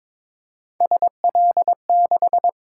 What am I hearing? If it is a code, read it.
Morse code, heard as a keyed tone, about 22 wpm, 700 Hz. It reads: SL6
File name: practice / 22wpm / Three Letter-Number Combinations - 22wpm